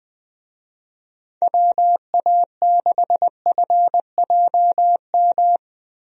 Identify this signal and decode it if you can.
Morse code, heard as a keyed tone, about 20 wpm, 700 Hz. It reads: WA6FJM
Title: USA Call Signs - 6 Characters - Repeated 2x - 20wpm